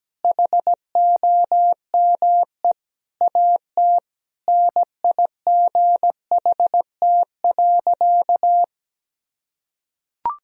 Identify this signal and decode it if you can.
Morse code, heard as a keyed tone, about 17 wpm, 700 Hz. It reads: HOME AT NIGHT.
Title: Sentences from Top 1000 Words - Review Part 3 - 17wpm